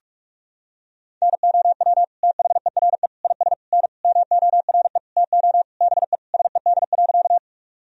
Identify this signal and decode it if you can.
Morse code, heard as a keyed tone, about 41 wpm, 700 Hz. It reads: NOW THERE IS N MORE TO BE SED.
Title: Sentences from Top 100 Words and Abbreviations 45wpm